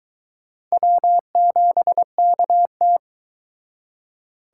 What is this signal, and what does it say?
Morse code, heard as a keyed tone, about 23 wpm, 700 Hz. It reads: W7KT